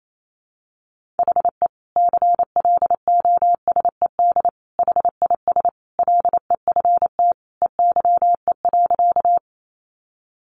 Morse code, heard as a keyed tone, about 28 wpm, 700 Hz. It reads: HE CLOSED HIS LEFT EYE.
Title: Sentences from Top 1400 Words 28wpm